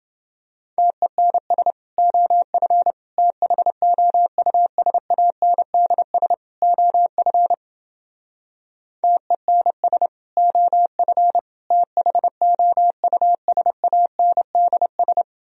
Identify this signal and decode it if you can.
Morse code, heard as a keyed tone, about 29 wpm, 700 Hz. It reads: TENS OF THOUSANDS OF TENS OF THOUSANDS